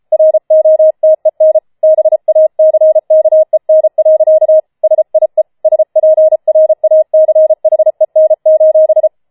Morse code, heard as a keyed tone, about 33 wpm, 630 Hz. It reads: ROTEN BACKEN. SIE SPRACHEN:
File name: Schneewittchen Teil5 34wpm